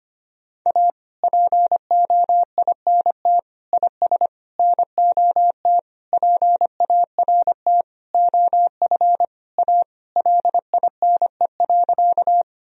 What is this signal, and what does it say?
Morse code, heard as a keyed tone, about 25 wpm, 700 Hz. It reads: A POINT IS NOT PART OF A LINE.
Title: Sentences from Top 1000 Words - Review Part 1 - 25wpm